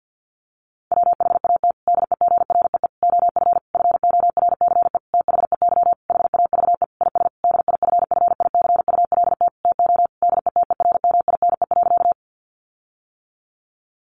Morse code, heard as a keyed tone, about 50 wpm, 700 Hz. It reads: WHAT DEGREE OF FORCE THEY HAVE IS DIFFICULT TO DETERMINE.
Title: Sentences from Top 1500 Words 50wpm